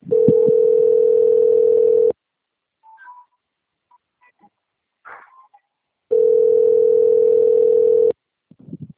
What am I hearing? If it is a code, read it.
Ringback tone (North America), 440 + 480 Hz.